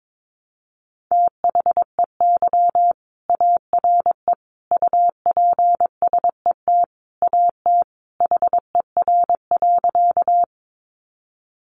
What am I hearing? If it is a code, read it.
Morse code, heard as a keyed tone, about 22 wpm, 700 Hz. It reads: THEY ARE UPSET AT HER.
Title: Sentences from Top 800 Words 22wpm